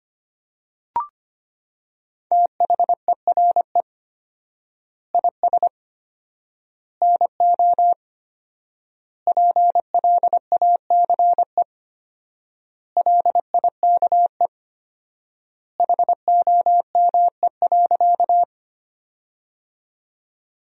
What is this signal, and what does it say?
Morse code, heard as a keyed tone, about 25 wpm, 700 Hz. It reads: THERE IS NO PLACE LIKE HOME.
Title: 4x Word-Spacing - Sentences from Top 200 Words - 25wpm